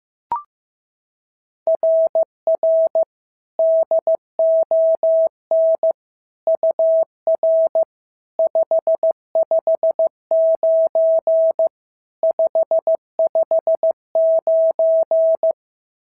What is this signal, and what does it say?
Morse code, heard as a keyed tone, about 14 wpm, 650 Hz. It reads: RR DON UR 559 559
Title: SOTA QSO Practice 15wpm